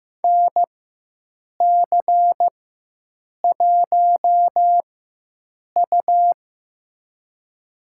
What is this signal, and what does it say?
Morse code, heard as a keyed tone, about 15 wpm, spaced as for 7 wpm, 700 Hz. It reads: NC1U
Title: USA Call Signs - 4 Characters - 4x Character Spacing - 15wpm